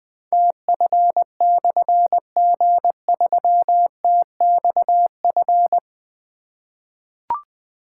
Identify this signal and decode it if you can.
Morse code, heard as a keyed tone, about 20 wpm, 700 Hz. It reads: TF/G3TXF
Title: Difficult Call Signs 20wpm